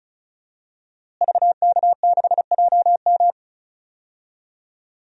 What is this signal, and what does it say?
Morse code, heard as a keyed tone, about 34 wpm, 700 Hz. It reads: VK6JM
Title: DXCC-100 Call Signs - 5 Characters - Repeated 2x - 35wpm